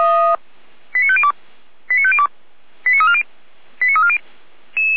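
Five-tone selective call (ZVEI), heard as quick runs of five tones, tones of about 70 ms. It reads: 89502 89502 89340 89340
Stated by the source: ZVEI five-tone selective call received on SDR.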